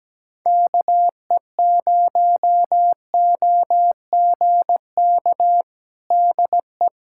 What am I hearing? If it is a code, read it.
Morse code, heard as a keyed tone, about 17 wpm, 700 Hz. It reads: KE0OGK DE
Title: USA Callsign DE Callsign - 6 Characters - 17wpm